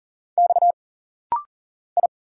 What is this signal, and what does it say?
Morse code, heard as a keyed tone, about 40 wpm, 700 Hz. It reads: X I
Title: Single Letters - Rapid-Fire 40wpm